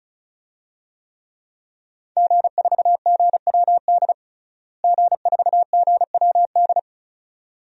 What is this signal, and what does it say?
Morse code, heard as a keyed tone, about 36 wpm, 700 Hz. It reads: G4GWD G4GWD
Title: DXCC-100 Call Signs - 5 Characters - Repeated 2x - 35wpm